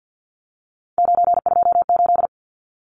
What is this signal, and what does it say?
Morse code, heard as a keyed tone, about 49 wpm, 700 Hz. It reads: 928
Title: Numbers - 3-Digit 50wpm